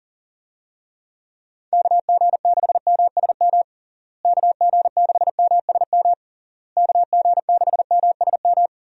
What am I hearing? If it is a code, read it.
Morse code, heard as a keyed tone, about 40 wpm, 700 Hz. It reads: KG6MSM KG6MSM KG6MSM
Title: USA Call Signs - 6 Characters - Repeated 3x - 40wpm